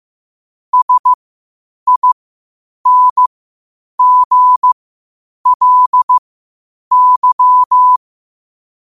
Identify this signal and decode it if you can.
Morse code, heard as a keyed tone, about 15 wpm, spaced as for 8 wpm, 1000 Hz. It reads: SINGLY
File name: qotdfiles / 8wpm